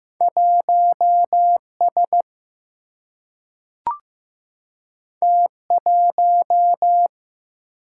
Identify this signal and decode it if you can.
Morse code, heard as a keyed tone, about 15 wpm, 700 Hz. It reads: 1S T1
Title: Single-Letter Single-Number 15wpm